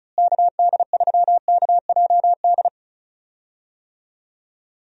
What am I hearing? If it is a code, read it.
Morse code, heard as a keyed tone, about 36 wpm, 700 Hz. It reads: KD3KJD
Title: USA Call Signs - 6 Characters - Repeated 2x - 35wpm